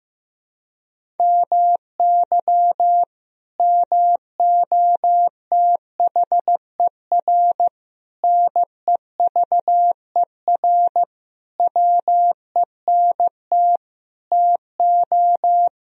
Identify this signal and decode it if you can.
Morse code, heard as a keyed tone, about 15 wpm, 700 Hz. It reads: MY MOTHER NEVER WENT TO
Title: Sentences from Top 1000 Words - Review Part 1 - 15wpm